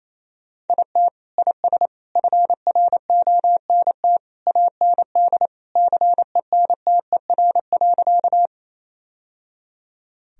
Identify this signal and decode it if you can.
Morse code, heard as a keyed tone, about 28 wpm, 700 Hz. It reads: IT IS FRONT AND CENTER.
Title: Sentences from Top 1000 Words - Review Part 2 - 28wpm